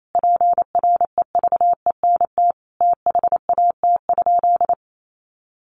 Morse code, heard as a keyed tone, about 28 wpm, 700 Hz. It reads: PREVENT THAT?
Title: Sentences from Top 1100 Words 28wpm